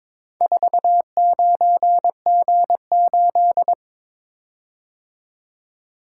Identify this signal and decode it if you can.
Morse code, heard as a keyed tone, about 22 wpm, 700 Hz. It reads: 49G8
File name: practice / 22wpm / Four Letter-Number Combinations - 22wpm